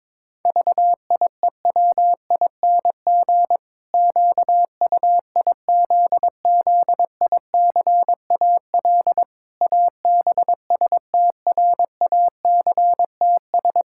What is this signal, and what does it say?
Morse code, heard as a keyed tone, about 22 wpm, 700 Hz. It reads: VIEWING QUIZZICAL ABSTRACTS
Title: Pangram Challenge v2 - 22wpm